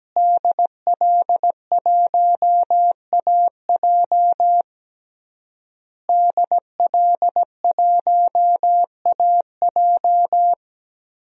Morse code, heard as a keyed tone, about 17 wpm, 700 Hz. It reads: DL1AJ DL1AJ
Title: DXCC-100 Call Signs - 5 Characters - Repeated 3x - 17wpm